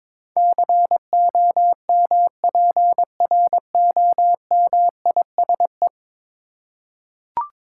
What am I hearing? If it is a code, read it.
Morse code, heard as a keyed tone, about 22 wpm, 700 Hz. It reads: COMPROMISE
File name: practice / 22wpm / 10 Letter Words - First 3 Letters Given - 22wpm